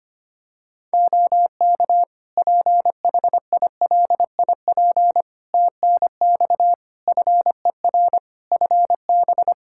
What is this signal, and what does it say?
Morse code, heard as a keyed tone, about 25 wpm, 700 Hz. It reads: OK PHILIP TNX FER FB
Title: Practice QSO - Exchange 4 - Ending - 25wpm